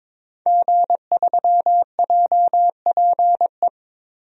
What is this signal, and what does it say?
Morse code, heard as a keyed tone, about 22 wpm, 700 Hz. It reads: G3JPE
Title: DXCC-100 Call Signs - 5 Characters - Repeated 3x - 22wpm